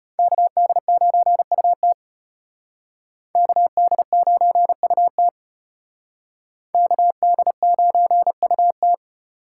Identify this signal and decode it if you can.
Morse code, heard as a keyed tone, about 31 wpm, 700 Hz. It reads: KD9UT KD9UT KD9UT
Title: Speed Racing - USA 2x2 Call Signs 25wpm